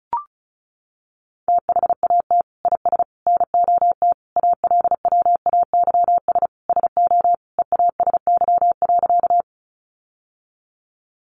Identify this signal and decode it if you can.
Morse code, heard as a keyed tone, about 34 wpm, 700 Hz. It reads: THAT IS NOT ALWAYS SO EASY.
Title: Sentences from Top 700 Words 35wpm